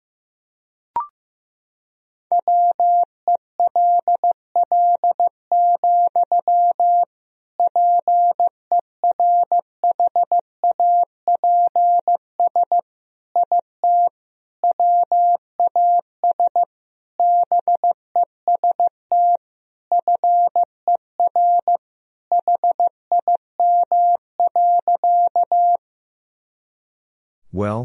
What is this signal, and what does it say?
Morse code, heard as a keyed tone, about 15 wpm, 700 Hz. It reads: WELL, PERHAPS IT WAS BEST FER HIM.
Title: Sentences from Top 500 Words and Abbreviations 15wpm